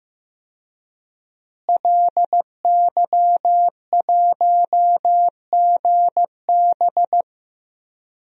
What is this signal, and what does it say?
Morse code, heard as a keyed tone, about 16 wpm, 700 Hz. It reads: LY1GB
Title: DXCC-100 Call Signs - 5 Characters - Repeated 3x - 15wpm